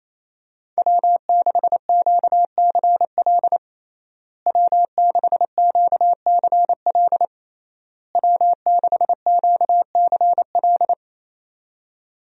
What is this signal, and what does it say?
Morse code, heard as a keyed tone, about 28 wpm, 700 Hz. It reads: W6QCL W6QCL W6QCL
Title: USA Call Signs - 5 Characters - Repeated 3x - 28wpm